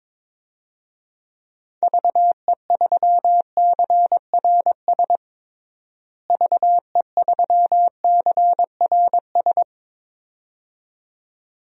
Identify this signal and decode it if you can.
Morse code, heard as a keyed tone, about 22 wpm, 700 Hz. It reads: VE3CRS VE3CRS
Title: DXCC-100 Call Signs - 6 Characters - Repeated 2x - 22wpm